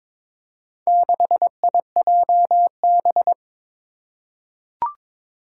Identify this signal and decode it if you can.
Morse code, heard as a keyed tone, about 22 wpm, 700 Hz. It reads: 6IJB
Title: Four Challenging Letter-Number Combinations 22wpm